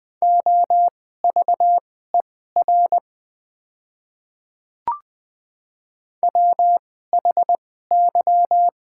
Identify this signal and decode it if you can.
Morse code, heard as a keyed tone, about 20 wpm, spaced as for 14 wpm, 700 Hz. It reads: OVER WHY
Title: 2x Character Spacing - Top 200 Words - 20wpm